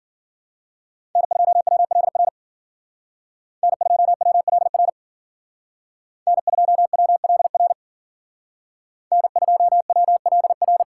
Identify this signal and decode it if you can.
Morse code, heard as a keyed tone, about 52 wpm, 700 Hz. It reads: N2WLR N2WLR N2WLR N2WLR